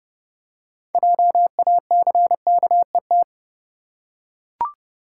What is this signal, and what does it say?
Morse code, heard as a keyed tone, about 30 wpm, 700 Hz. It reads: JACKET
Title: Six Letter Words - 30wpm